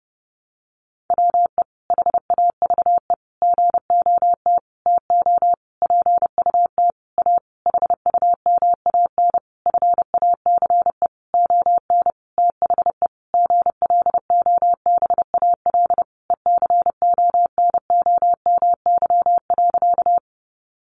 Morse code, heard as a keyed tone, about 30 wpm, 700 Hz. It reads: WE HAVE GOT TO PUT A HUMAN FACE ON THE GLOBAL ECONOMY.
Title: Sentences from Top 1200 Words 30wpm